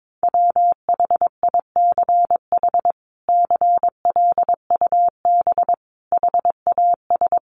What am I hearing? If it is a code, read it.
Morse code, heard as a keyed tone, about 22 wpm, 700 Hz. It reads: WHICH CLUB HAS
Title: Sentences from Top 1200 Words 22wpm